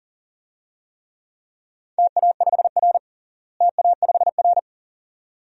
Morse code, heard as a keyed tone, about 37 wpm, 700 Hz. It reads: TA5R TA5R
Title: DXCC-100 Call Signs - 4 Characters - Repeated 2x - 40wpm